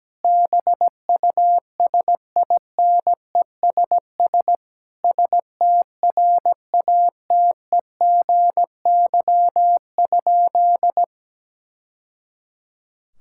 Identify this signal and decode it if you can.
Morse code, heard as a keyed tone, about 17 wpm, 700 Hz. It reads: BUSINESS STRATEGY?
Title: Sentences from Top 1000 Words - Review Part 6 - 17wpm